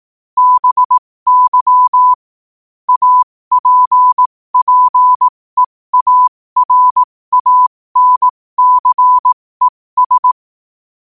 Morse code, heard as a keyed tone, about 18 wpm, spaced as for 15 wpm, 1000 Hz. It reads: BY APPEARANCES
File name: qotdfiles / 15wpm